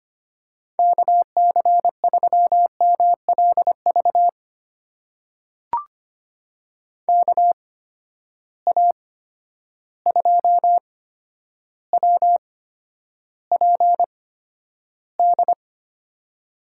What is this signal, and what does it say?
Morse code, heard as a keyed tone, about 25 wpm, spaced as for 11 wpm, 700 Hz. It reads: KC3MLV KA2WPD
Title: USA Call Signs - 6 Characters - 8x Character Spacing - 25wpm